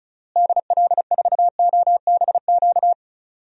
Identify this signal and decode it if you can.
Morse code, heard as a keyed tone, about 36 wpm, 700 Hz. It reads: DL4OBQ
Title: USA Call Signs - 6 Characters - Repeated 3x - 35wpm